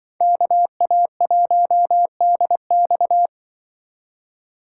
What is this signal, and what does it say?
Morse code, heard as a keyed tone, about 25 wpm, 700 Hz. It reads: KA1DX
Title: Speed Racing - USA 2x2 Call Signs 20wpm